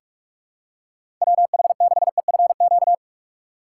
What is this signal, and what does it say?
Morse code, heard as a keyed tone, about 41 wpm, 700 Hz. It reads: WH6EFQ
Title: USA Call Signs - 2x Character Spacing - 45wpm